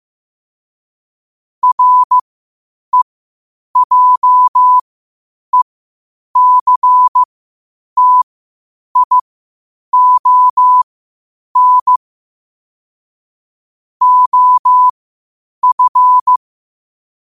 Morse code, heard as a keyed tone, about 15 wpm, spaced as for 8 wpm, 1000 Hz. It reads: REJECTION OF
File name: qotdfiles / 8wpm